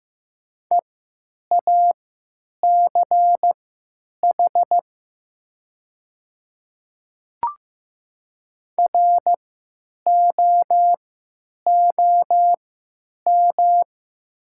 Morse code, heard as a keyed tone, about 15 wpm, spaced as for 9 wpm, 700 Hz. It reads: EACH ROOM
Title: 3x Character Spacing - Top 300 Words - 15wpm